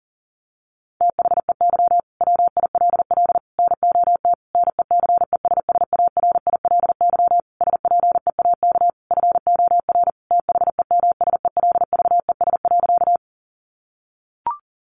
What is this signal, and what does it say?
Morse code, heard as a keyed tone, about 40 wpm, 700 Hz. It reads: THEY WILL NOT NECESSARILY SPEAK FOR THEMSELVES.
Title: Sentences from Top 1400 Words 40wpm